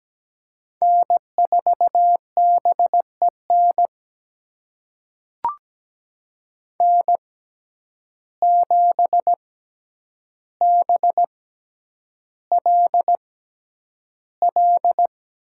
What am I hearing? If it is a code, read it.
Morse code, heard as a keyed tone, about 17 wpm, spaced as for 9 wpm, 700 Hz. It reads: N4BEN N7BLL
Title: USA Call Signs - 5 Characters - 6x Character Spacing - 17wpm